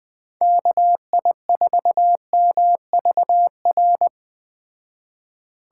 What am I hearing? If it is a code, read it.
Morse code, heard as a keyed tone, about 20 wpm, 700 Hz. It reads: KI4MVR